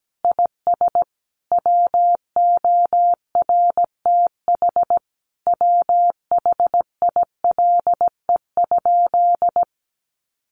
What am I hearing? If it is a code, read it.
Morse code, heard as a keyed tone, about 17 wpm, 700 Hz. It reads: IS WORTH WHILE?
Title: Sentences from Top 800 Words 17wpm